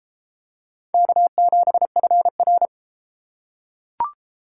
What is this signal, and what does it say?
Morse code, heard as a keyed tone, about 31 wpm, 700 Hz. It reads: K7FR